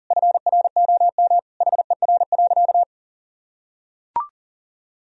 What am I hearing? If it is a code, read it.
Morse code, heard as a keyed tone, about 40 wpm, 700 Hz. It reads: FROM HER.
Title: Sentences from Top 1000 Words - Review Part 2 - 40wpm